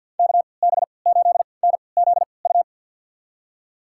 Morse code, heard as a keyed tone, about 52 wpm, spaced as for 29 wpm, 700 Hz. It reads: KB8NZU
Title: USA Call Signs - 6 Characters - 3x Character Spacing - 50wpm